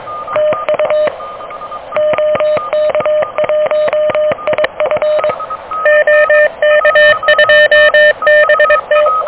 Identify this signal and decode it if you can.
Morse code, heard as a keyed tone, about 24 wpm, 610 Hz. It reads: TU OK1IF OK2BT
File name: OK QRP 2023 / OK QRP 2023